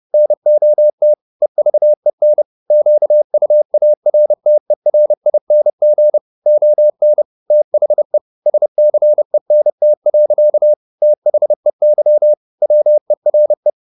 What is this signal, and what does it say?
Morse code, heard as a keyed tone, about 30 wpm, 600 Hz. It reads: NOT EVEN QUARTERING ON THE SCENT. THEY WERE